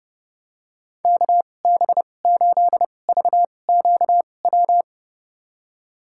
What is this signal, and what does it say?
Morse code, heard as a keyed tone, about 30 wpm, spaced as for 22 wpm, 700 Hz. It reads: KB8VQW
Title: USA Call Signs - 6 Characters - 2x Character Spacing - 30wpm